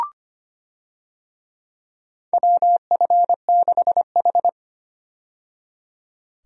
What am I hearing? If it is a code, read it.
Morse code, heard as a keyed tone, about 25 wpm, 700 Hz. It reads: WF6H
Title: USA Call Signs - 4 Characters - 25wpm